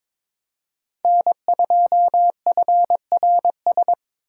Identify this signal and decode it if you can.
Morse code, heard as a keyed tone, about 27 wpm, 700 Hz. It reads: N2FRS
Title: Speed Racing - USA 1x3 Call Signs 22wpm